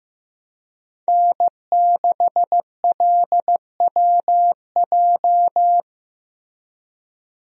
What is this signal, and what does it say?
Morse code, heard as a keyed tone, about 15 wpm, 700 Hz. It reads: N6LWJ
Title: USA Call Signs - 5 Characters - Repeated 3x - 15wpm